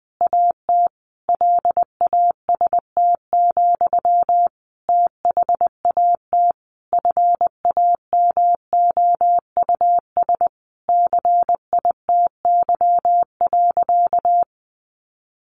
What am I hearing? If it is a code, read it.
Morse code, heard as a keyed tone, about 20 wpm, 700 Hz. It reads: AT LAST, THAT FAMOUS CITY.
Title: Sentences from Top 1500 Words 20wpm